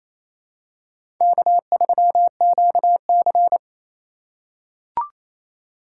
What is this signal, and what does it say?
Morse code, heard as a keyed tone, about 28 wpm, 700 Hz. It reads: K3QC
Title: USA Call Signs - 8x Character Spacing - 28wpm